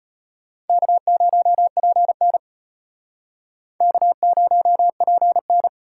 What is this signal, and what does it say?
Morse code, heard as a keyed tone, about 34 wpm, 700 Hz. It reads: K0PN K0PN